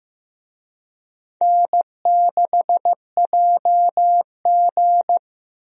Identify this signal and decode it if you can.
Morse code, heard as a keyed tone, about 15 wpm, 700 Hz. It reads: N6JG
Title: USA Call Signs - 4 Characters - Repeated 2x - 15wpm